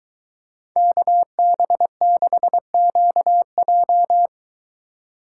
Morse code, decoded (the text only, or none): KB6QJ